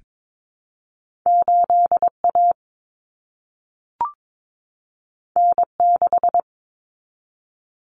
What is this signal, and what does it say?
Morse code, heard as a keyed tone, about 22 wpm, 700 Hz. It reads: 8A N6
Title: Common Cut Numbers - 2-Digit 22wpm